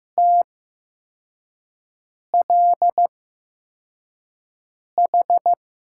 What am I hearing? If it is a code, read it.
Morse code, heard as a keyed tone, about 15 wpm, spaced as for 4 wpm, 700 Hz. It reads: TLH
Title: USA Call Signs - 6 Characters - 8x Character Spacing - 15wpm